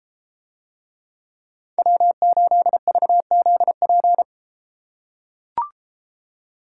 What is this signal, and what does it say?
Morse code, heard as a keyed tone, about 32 wpm, 700 Hz. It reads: W8VZP